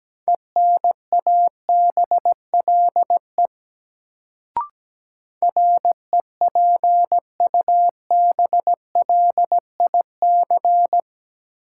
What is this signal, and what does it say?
Morse code, heard as a keyed tone, about 17 wpm, 700 Hz. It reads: ENABLE REPUBLIC
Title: Sentences from Top 1500 Words - New Word Drill - 17wpm